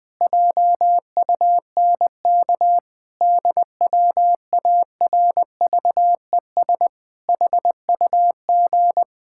Morse code, heard as a keyed tone, about 20 wpm, 700 Hz. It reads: JUNK DWARVES HUG
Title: Pangram Challenge v2 - 20wpm